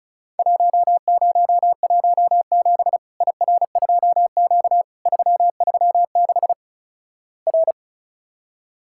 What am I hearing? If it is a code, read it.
Morse code, heard as a keyed tone, about 36 wpm, 700 Hz. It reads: 1017 IR2Q 336 R
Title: Worked All Europe DX Contest - QTC Exchange - 35wpm